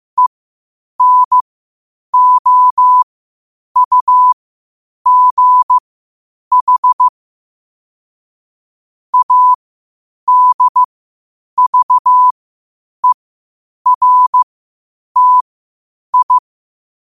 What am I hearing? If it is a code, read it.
Morse code, heard as a keyed tone, about 15 wpm, spaced as for 8 wpm, 1000 Hz. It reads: ENOUGH ADVERTI
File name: qotdfiles / 8wpm